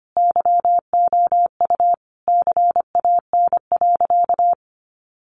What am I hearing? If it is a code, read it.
Morse code, heard as a keyed tone, about 25 wpm, 700 Hz. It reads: YOU CAN.
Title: Sentences from Top 1200 Words 25wpm